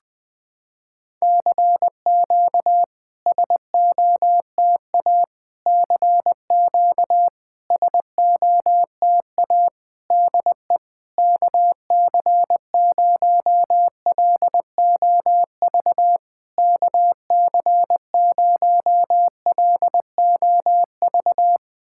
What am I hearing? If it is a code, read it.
Morse code, heard as a keyed tone, about 20 wpm, 700 Hz. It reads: CQ SOTA CQ SOTA DE KC0LOV KC0LOV
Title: SOTA QSO Practice 20wpm